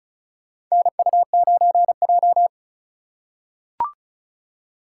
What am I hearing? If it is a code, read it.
Morse code, heard as a keyed tone, about 36 wpm, 700 Hz. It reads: NU9J
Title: USA Call Signs - 4 Characters - 3x Character Spacing - 35wpm